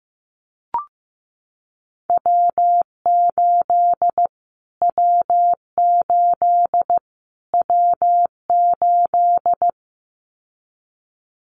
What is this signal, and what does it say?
Morse code, heard as a keyed tone, about 15 wpm, 700 Hz. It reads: W8 W8 W8
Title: Callsign Prefixes for Top 25 DXCC - Repeated 3x - 15wpm